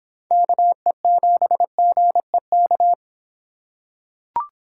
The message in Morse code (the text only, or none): KE7GEK